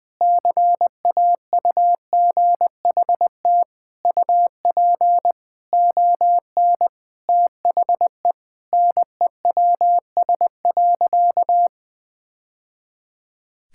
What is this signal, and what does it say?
Morse code, heard as a keyed tone, about 20 wpm, 700 Hz. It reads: CAUGHT UP ON THE NEWS.
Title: Sentences with 3-Word Phrases - 20wpm